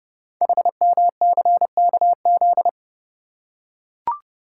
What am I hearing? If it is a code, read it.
Morse code, heard as a keyed tone, about 30 wpm, 690 Hz. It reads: HMCKZ
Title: Code Groups - 5 Letters - 30wpm